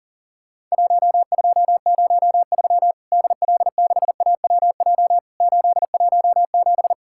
Morse code, heard as a keyed tone, about 40 wpm, 700 Hz. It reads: 1203 DL6AWJ 817
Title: Worked All Europe DX Contest - QTC Exchange - 40wpm